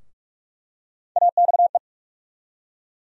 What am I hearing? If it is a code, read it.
Morse code, heard as a keyed tone, about 45 wpm, 700 Hz. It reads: AXE
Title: USA Call Sign Postfixes - 3 Letters 45wpm